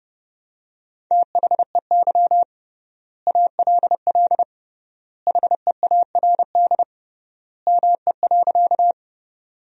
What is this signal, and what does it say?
Morse code, heard as a keyed tone, about 32 wpm, 700 Hz. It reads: THEY ALL HEARD ME.
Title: 3x Word-Spacing - Sentences from Top 400 Words - 30wpm